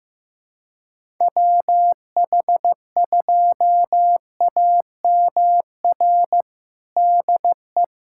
Morse code, heard as a keyed tone, about 16 wpm, 700 Hz. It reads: WH2AMR DE
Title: USA Callsign DE Callsign - 6 Characters - 15wpm